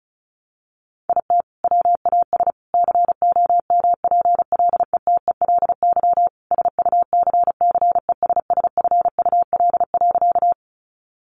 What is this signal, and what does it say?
Morse code, heard as a keyed tone, about 34 wpm, 700 Hz. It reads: IT WAS COMPLETELY SUCCESSFUL.
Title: Sentences from Top 1100 Words 35wpm